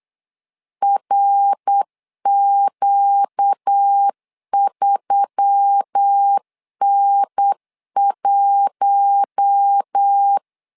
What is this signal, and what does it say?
Morse code, heard as a keyed tone, about 8 wpm, 800 Hz. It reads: RQ3N1
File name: LEZIONI / 4 Q1S3 8wpm 3037